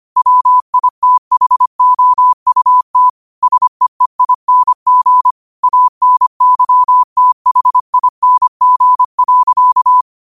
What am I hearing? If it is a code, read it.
Morse code, heard as a keyed tone, about 25 wpm, 1000 Hz. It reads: WITHOUT SEEING ANYTHING.
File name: qotdfiles / 25wpm